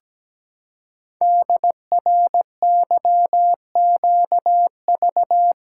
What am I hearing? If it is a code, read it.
Morse code, heard as a keyed tone, about 17 wpm, 700 Hz. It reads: DRYQV
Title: Code Groups - 5 Letters - 17wpm